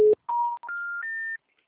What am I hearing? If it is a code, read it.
Special information tone, with noise.